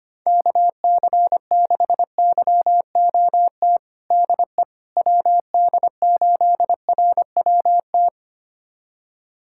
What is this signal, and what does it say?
Morse code, heard as a keyed tone, about 25 wpm, 700 Hz. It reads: KC6YOT DE WD8RWT
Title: USA Callsign DE Callsign 25wpm